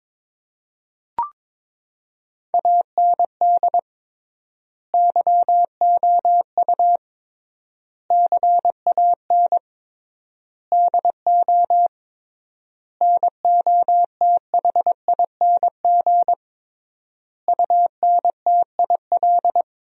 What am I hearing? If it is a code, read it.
Morse code, heard as a keyed tone, about 22 wpm, 700 Hz. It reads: AND YOU CAN DO NOTHING UNTIL
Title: 3x Word-Spacing - Sentences from Top 500 Words - 22wpm